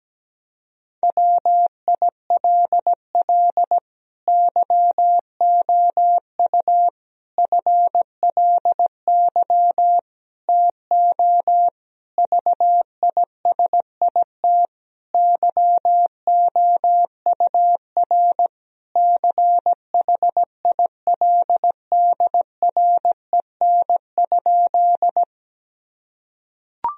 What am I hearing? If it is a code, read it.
Morse code, heard as a keyed tone, about 17 wpm, 700 Hz. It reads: WILL YOU FLY TO VISIT YOUR CHILDREN?
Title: Sentences from Top 1000 Words - Review Part 2 - 17wpm